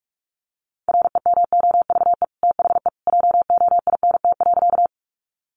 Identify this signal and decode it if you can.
Morse code, heard as a keyed tone, about 44 wpm, 700 Hz. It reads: REMOVE THE JOINT.
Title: Sentences from Top 1300 Words 45wpm